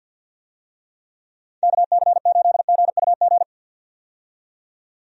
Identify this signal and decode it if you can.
Morse code, heard as a keyed tone, about 52 wpm, 700 Hz. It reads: KC8GUG
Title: USA Call Signs - 6 Characters - 50wpm